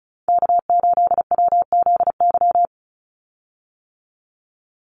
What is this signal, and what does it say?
Morse code, heard as a keyed tone, about 34 wpm, 700 Hz. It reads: K8WZY
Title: USA Call Signs 35wpm